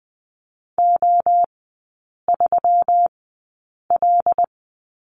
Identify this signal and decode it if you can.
Morse code, heard as a keyed tone, about 20 wpm, spaced as for 8 wpm, 700 Hz. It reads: O3L
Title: 121 Lesson 20e - Number 3 - Three Characters 20wpm